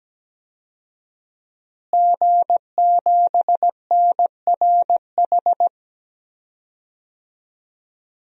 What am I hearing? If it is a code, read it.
Morse code, heard as a keyed tone, about 17 wpm, 700 Hz. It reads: G7NRH
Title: DXCC-100 Call Signs - 5 Characters - Repeated 3x - 17wpm